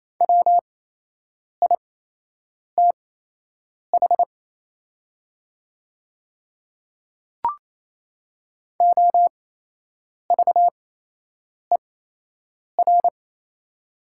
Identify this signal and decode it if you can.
Morse code, heard as a keyed tone, about 28 wpm, spaced as for 8 wpm, 700 Hz. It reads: WITH OVER
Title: 8x Character Spacing - Top 200 Words - 28wpm